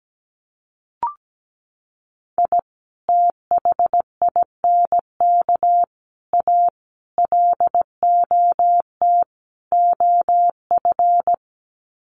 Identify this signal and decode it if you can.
Morse code, heard as a keyed tone, about 17 wpm, 700 Hz. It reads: I THINK A LOT OF
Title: Sets of 5 Words 17wpm